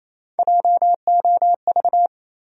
Morse code, heard as a keyed tone, about 28 wpm, 700 Hz. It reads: JOV
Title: USA Call Sign Postfixes - 3 Letters 28wpm